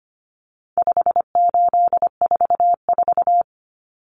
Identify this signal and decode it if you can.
Morse code, heard as a keyed tone, about 25 wpm, 700 Hz. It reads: 5844
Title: Common Cut Numbers - 4-Digit 25wpm